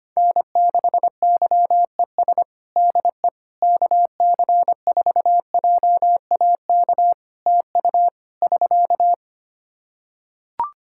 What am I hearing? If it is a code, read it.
Morse code, heard as a keyed tone, about 25 wpm, 700 Hz. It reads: N6YES DE KC4JAK TU <SK>
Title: K1USN SST QSO Practice 25wpm